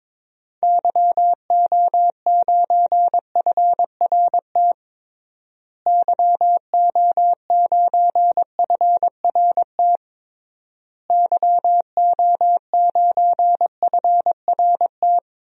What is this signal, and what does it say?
Morse code, heard as a keyed tone, about 21 wpm, 700 Hz. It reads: YO9FRT YO9FRT YO9FRT
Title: DXCC-100 Call Signs - 6 Characters - Repeated 3x - 22wpm